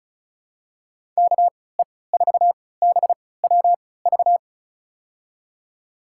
Morse code, heard as a keyed tone, about 36 wpm, spaced as for 20 wpm, 700 Hz. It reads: KE4BWV
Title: USA Call Signs - 6 Characters - 3x Character Spacing - 35wpm